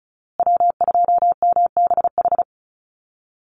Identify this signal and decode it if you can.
Morse code, heard as a keyed tone, about 34 wpm, 700 Hz. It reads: W2MBH
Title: USA Call Signs Repeated 2x 35wpm